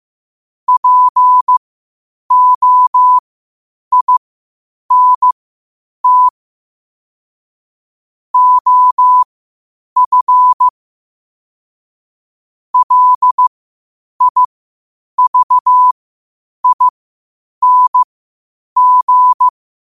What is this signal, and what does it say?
Morse code, heard as a keyed tone, about 15 wpm, spaced as for 8 wpm, 1000 Hz. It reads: POINT OF LIVING
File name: qotdfiles / 8wpm